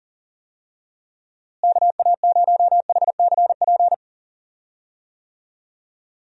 Morse code, heard as a keyed tone, about 40 wpm, 700 Hz. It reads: KA0HCP
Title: USA Call Signs - 6 Characters - Repeated 3x - 40wpm